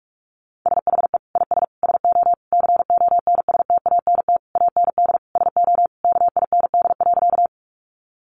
Morse code, heard as a keyed tone, about 44 wpm, 700 Hz. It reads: SHE IS SO CONSTANT AND SO KIND.
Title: Sentences from Top 1500 Words 45wpm